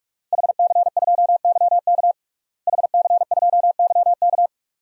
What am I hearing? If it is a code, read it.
Morse code, heard as a keyed tone, about 46 wpm, 700 Hz. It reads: HC2YK HC2YK
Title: DXCC-100 Call Signs - 5 Characters - Repeated 2x - 45wpm